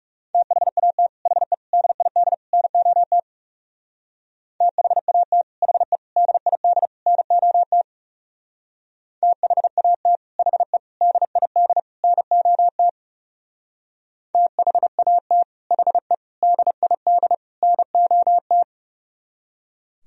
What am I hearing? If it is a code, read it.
Morse code, heard as a keyed tone, about 37 wpm, 700 Hz. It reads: THAT HE DID NOT THAT HE DID NOT THAT HE DID NOT THAT HE DID NOT
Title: Speed-Racing - Sets of 4 Words 30wpm